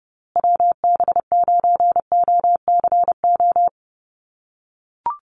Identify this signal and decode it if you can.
Morse code, heard as a keyed tone, about 30 wpm, 700 Hz. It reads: WB9OCO
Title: USA Call Signs Repeated 3x 30wpm